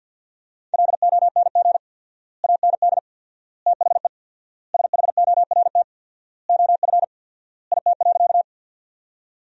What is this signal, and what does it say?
Morse code, heard as a keyed tone, about 50 wpm, 700 Hz. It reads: LONG AND THE SHORT OF IT.
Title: 4x Word-Spacing - Sentences from Top 400 Words - 50wpm